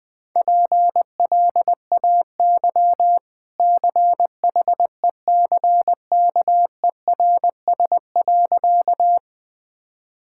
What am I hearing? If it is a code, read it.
Morse code, heard as a keyed tone, about 21 wpm, 700 Hz. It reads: PLAY CHECKERS.
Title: Easy Sentences with Hints - 20wpm